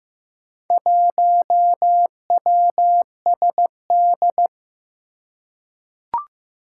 Morse code, heard as a keyed tone, about 15 wpm, 700 Hz. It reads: 1WSD
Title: Four Challenging Letter-Number Combinations 15wpm